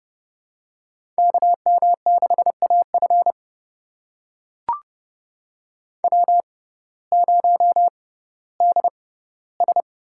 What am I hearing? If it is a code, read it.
Morse code, heard as a keyed tone, about 32 wpm, spaced as for 17 wpm, 700 Hz. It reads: KM6AF W0DS
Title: USA Call Signs - 5 Characters - 6x Character Spacing - 30wpm